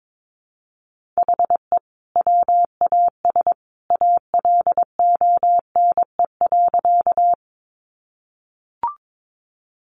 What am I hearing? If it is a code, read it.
Morse code, heard as a keyed tone, about 22 wpm, 700 Hz. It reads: HE WAS ALONE.
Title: Sentences from Top 1000 Words 22wpm